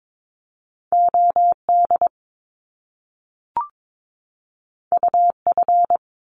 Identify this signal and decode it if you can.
Morse code, heard as a keyed tone, about 22 wpm, 700 Hz. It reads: OD UF
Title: Top Two Letter Combinations in Words 22wpm